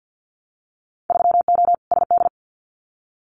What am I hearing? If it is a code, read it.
Morse code, heard as a keyed tone, about 50 wpm, 700 Hz. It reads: 3O SD
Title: Winter Field Day Exchanges 50wpm